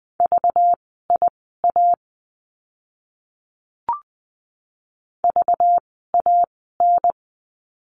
Morse code, heard as a keyed tone, about 20 wpm, spaced as for 14 wpm, 700 Hz. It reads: VIA VAN
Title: 192 Lesson 31f - Letter v - Words 20wpm